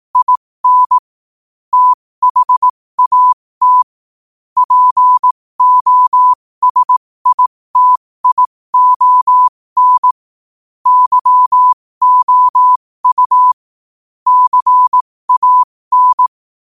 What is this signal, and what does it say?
Morse code, heard as a keyed tone, about 18 wpm, spaced as for 15 wpm, 1000 Hz. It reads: IN THAT POSITION YOU CAN
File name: qotdfiles / 15wpm